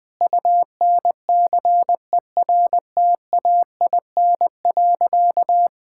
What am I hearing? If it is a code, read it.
Morse code, heard as a keyed tone, about 21 wpm, 700 Hz. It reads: UNCERTAIN.
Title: Sentences with 3-Word Phrases - 20wpm